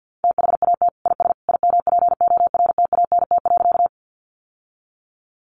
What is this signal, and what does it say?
Morse code, heard as a keyed tone, about 50 wpm, 700 Hz. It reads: THAT IS IMPORTANT.
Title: Sentences from Top 600 Words 50wpm